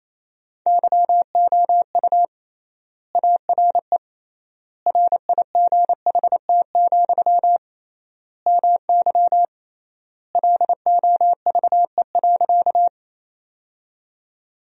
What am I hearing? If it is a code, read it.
Morse code, heard as a keyed tone, about 28 wpm, 700 Hz. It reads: YOU ARE RIGHT, MY LOVE.
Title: 3x Word-Spacing - Sentences from Top 400 Words - 28wpm